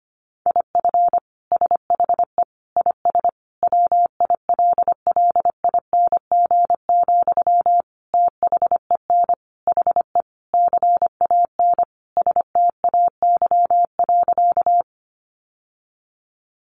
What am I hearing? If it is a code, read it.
Morse code, heard as a keyed tone, about 25 wpm, 700 Hz. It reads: IF SHE IS WILLING, THEN HE CAN STAY.
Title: Sentences from Top 1000 Words 25wpm